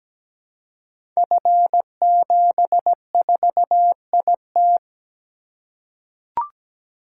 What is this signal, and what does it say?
Morse code, heard as a keyed tone, about 17 wpm, 700 Hz. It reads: F74IT
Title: Code Groups - 5 Letter:Numbers - 17wpm